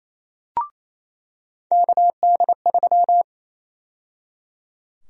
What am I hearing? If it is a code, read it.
Morse code, heard as a keyed tone, about 28 wpm, 700 Hz. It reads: KD3
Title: USA Call Sign Prefixes - 28wpm